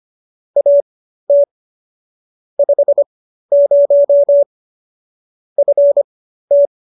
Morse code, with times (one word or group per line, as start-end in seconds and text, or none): 0.6-1.4 AT
2.6-4.4 50
5.6-6.7 FT